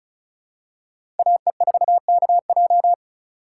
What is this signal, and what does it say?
Morse code, heard as a keyed tone, about 36 wpm, 700 Hz. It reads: AE4KJ
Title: USA Call Signs - 5 Characters - 3x Character Spacing - 35wpm